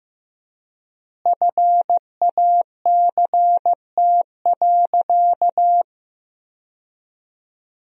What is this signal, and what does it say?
Morse code, heard as a keyed tone, about 15 wpm, 700 Hz. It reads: FACT.
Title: 4x Word-Spacing - Sentences from Top 500 Words - 15wpm